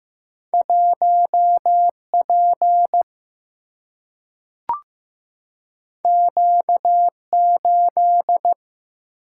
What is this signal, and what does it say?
Morse code, heard as a keyed tone, about 15 wpm, 700 Hz. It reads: 1 P Q 8
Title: Single-Letter Single-Number 15wpm